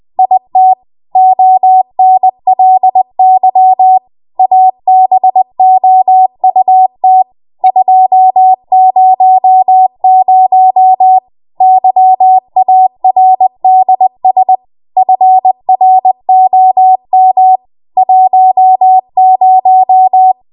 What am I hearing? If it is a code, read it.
Morse code, heard as a keyed tone, about 20 wpm, 750 Hz. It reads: IT ONLY ABOUT 200 YARDS FROM 10
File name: Morse / 20WPM